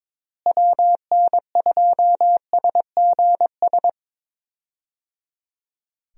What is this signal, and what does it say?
Morse code, heard as a keyed tone, about 22 wpm, 700 Hz. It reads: WN2SGS
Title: USA Call Signs - 6 Characters - Repeated 3x - 22wpm